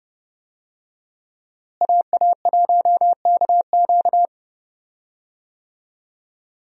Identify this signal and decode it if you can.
Morse code, heard as a keyed tone, about 32 wpm, 700 Hz. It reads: AA1KQ
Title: USA Call Signs - 5 Characters - Repeated 3x - 30wpm